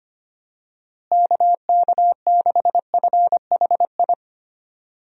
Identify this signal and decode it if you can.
Morse code, heard as a keyed tone, about 25 wpm, 700 Hz. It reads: KK6FHI
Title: USA Call Signs - 6 Characters - Repeated 2x - 25wpm